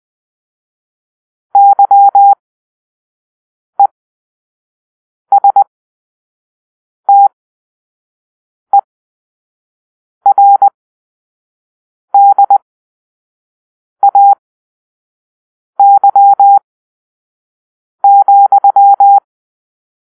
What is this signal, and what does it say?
Morse code, heard as a keyed tone, about 20 wpm, spaced as for 5 wpm, 800 Hz. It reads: YESTERDAY,